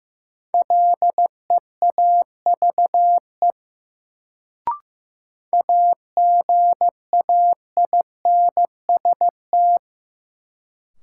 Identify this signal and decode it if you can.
Morse code, heard as a keyed tone, about 15 wpm, 700 Hz. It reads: LEAVE AGAINST
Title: Sentences from Top 400 Words - New Word Drill - 15wpm